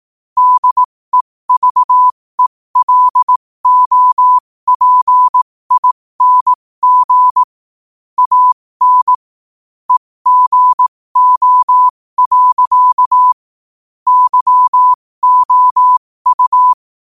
Morse code, heard as a keyed tone, about 18 wpm, spaced as for 15 wpm, 1000 Hz. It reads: DEVELOPING AN EGO. YOU
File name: qotdfiles / 15wpm